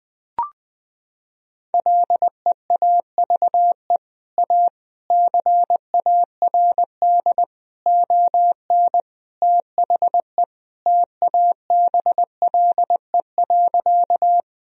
Morse code, heard as a keyed tone, about 21 wpm, 700 Hz. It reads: LEAVE A CARD ON THE TABLE.
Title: Sentences from Top 1000 Words - Review Part 4 - 20wpm